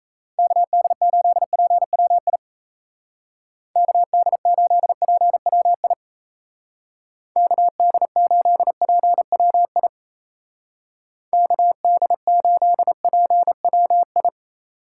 Morse code, heard as a keyed tone, about 34 wpm, 700 Hz. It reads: KD8PWI KD8PWI KD8PWI KD8PWI
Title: Speed Racing - USA 2x3 Call Signs 28wpm